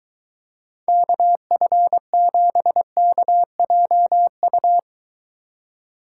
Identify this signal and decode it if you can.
Morse code, heard as a keyed tone, about 22 wpm, 700 Hz. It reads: KF7KJU